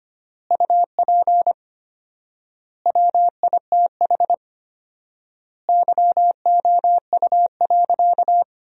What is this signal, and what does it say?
Morse code, heard as a keyed tone, about 25 wpm, 700 Hz. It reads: UP WITH YOU.
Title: 4x Word-Spacing - Sentences from Top 300 Words - 25wpm